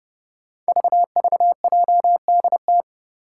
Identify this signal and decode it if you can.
Morse code, heard as a keyed tone, about 30 wpm, 700 Hz. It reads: VVJDT
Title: Five Challenging Letter-Number Combinations 30wpm